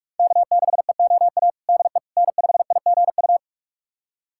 Morse code, heard as a keyed tone, about 46 wpm, 700 Hz. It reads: K6EOA DE N5IGU
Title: USA Callsign DE Callsign - 5 Characters - 45wpm